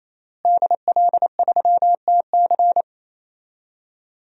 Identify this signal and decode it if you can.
Morse code, heard as a keyed tone, about 27 wpm, 700 Hz. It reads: DL3TC
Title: DXCC-100 Call Signs - 5 Characters - Repeated 3x - 28wpm